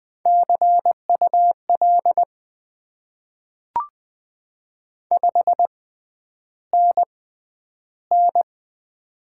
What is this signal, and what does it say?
Morse code, heard as a keyed tone, about 20 wpm, spaced as for 10 wpm, 700 Hz. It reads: CUL 5NN
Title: Top 90 QSO Elements - 6x Character Spacing - 20wpm